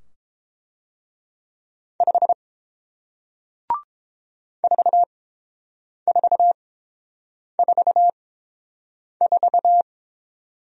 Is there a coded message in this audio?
Yes: Morse code from 2.0 to 9.8 s, heard as a keyed tone, about 30 wpm, spaced as for 7 wpm, 700 Hz. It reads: 5 4444